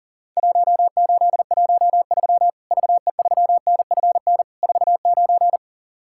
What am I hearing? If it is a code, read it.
Morse code, heard as a keyed tone, about 40 wpm, 700 Hz. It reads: 1813 VE3NFN 49
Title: Worked All Europe DX Contest - QTC Exchange - 40wpm